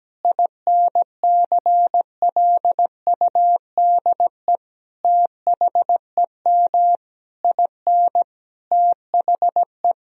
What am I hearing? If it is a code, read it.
Morse code, heard as a keyed tone, about 17 wpm, 700 Hz. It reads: INCLUDE THEM IN THE
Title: Sentences from Top 1000 Words - Review Part 3 - 17wpm